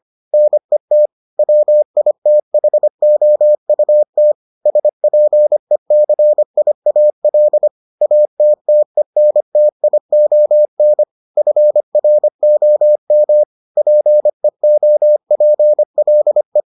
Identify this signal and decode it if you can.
Morse code, heard as a keyed tone, about 25 wpm, 600 Hz. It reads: NET WITHOUT SPECIAL ATTENTION FROM PEOPLE